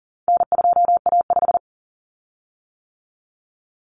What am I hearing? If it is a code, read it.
Morse code, heard as a keyed tone, about 40 wpm, 700 Hz. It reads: N2A5
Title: Common Cut Numbers - 4-Digit 40wpm